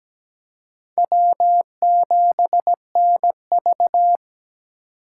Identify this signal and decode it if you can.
Morse code, heard as a keyed tone, about 17 wpm, 700 Hz. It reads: W7NV
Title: USA Call Signs - 4 Characters - 2x Character Spacing - 17wpm